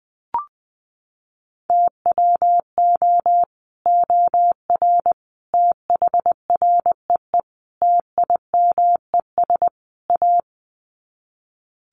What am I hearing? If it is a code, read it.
Morse code, heard as a keyed tone, about 20 wpm, 700 Hz. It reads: TWO OR THREE TIMES A
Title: Sets of 5 Words Encore2 20wpm